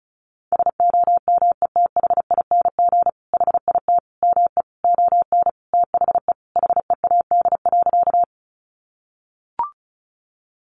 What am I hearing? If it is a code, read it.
Morse code, heard as a keyed tone, about 34 wpm, 700 Hz. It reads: SOMETHING HIT ME ON THE HEAD.
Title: Sentences from Top 1200 Words 35wpm